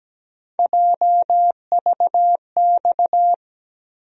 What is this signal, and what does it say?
Morse code, heard as a keyed tone, about 17 wpm, 700 Hz. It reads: JVX
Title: USA Call Sign Postfixes - 3 Letters 17wpm